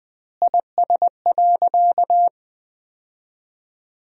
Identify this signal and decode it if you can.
Morse code, heard as a keyed tone, about 20 wpm, 700 Hz. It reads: IS.
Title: 2x Word-Spacing - Sentences from Top 100 Words - 20wpm